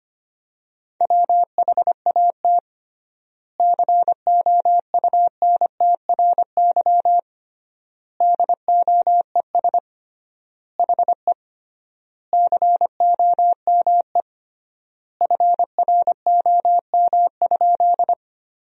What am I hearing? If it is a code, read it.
Morse code, heard as a keyed tone, about 25 wpm, 700 Hz. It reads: WHAT COUNTRY DOES HE COME FROM?
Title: 3x Word-Spacing - Sentences from Top 300 Words - 25wpm